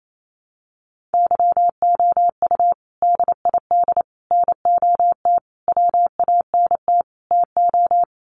Morse code, heard as a keyed tone, about 28 wpm, 700 Hz. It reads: YOU DID NOT WANT TO
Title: Sets of 5 Words Encore2 28wpm